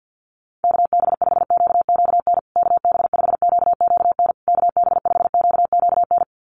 Morse code, heard as a keyed tone, about 50 wpm, 700 Hz. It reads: KB5QQN KB5QQN KB5QQN
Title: USA Call Signs Repeated 3x 50wpm